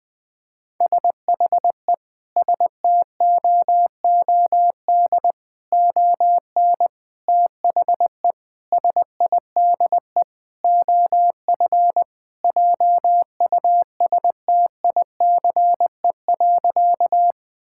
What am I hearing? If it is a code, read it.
Morse code, heard as a keyed tone, about 20 wpm, 700 Hz. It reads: SHE STOOD ON THE SIDE OF JUSTICE.
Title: Sentences with 4-Word Phrases - Spoken Beginning - 20wpm